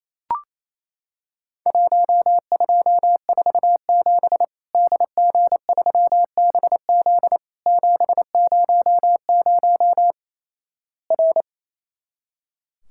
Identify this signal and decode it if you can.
Morse code, heard as a keyed tone, about 28 wpm, 700 Hz. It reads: 1247 DG3BZ 700 R
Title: Worked All Europe DX Contest - QTC Exchange - 28wpm